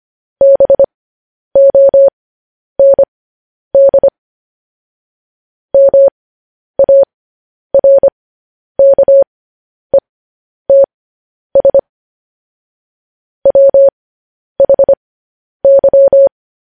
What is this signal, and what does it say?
Morse code, heard as a keyed tone, about 25 wpm, spaced as for 10 wpm, 550 Hz. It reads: BOND MARKETS WHY
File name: podcast / News headlines-1x-25x10